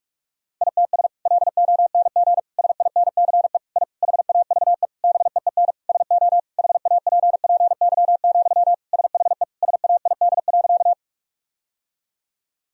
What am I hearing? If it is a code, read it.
Morse code, heard as a keyed tone, about 46 wpm, 700 Hz. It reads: ITS LONG SINCE I HAVE BEEN SO HAPPY, SHE SAID.
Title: Sentences from Top 1000 Words - Review Part 1 - 45wpm